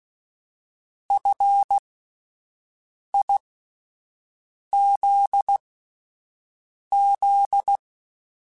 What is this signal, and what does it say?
Morse code, heard as a keyed tone, about 16 wpm, spaced as for 5 wpm, 800 Hz. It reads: FIZZ
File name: Alphabetical / Alphabetical mp3s / Fs @5